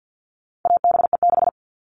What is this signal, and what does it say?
Morse code, heard as a keyed tone, about 50 wpm, 700 Hz. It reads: ABE6
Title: All Cut Numbers - 4-Digit 50wpm